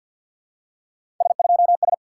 Morse code, heard as a keyed tone, about 48 wpm, 700 Hz. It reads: S2S
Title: SOTA QSO Practice 50wpm